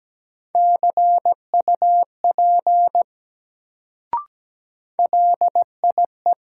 Morse code, heard as a keyed tone, about 17 wpm, 700 Hz. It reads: CUP LIE
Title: Sentences from Top 1600 Words - New Word Drill - 17wpm